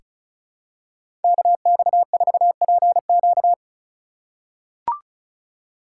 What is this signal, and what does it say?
Morse code, heard as a keyed tone, about 36 wpm, 700 Hz. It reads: KX4PQ
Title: K1USN SST QSO Practice - 35wpm